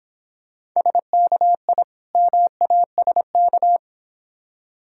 Morse code, heard as a keyed tone, about 26 wpm, 690 Hz. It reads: SKI MASK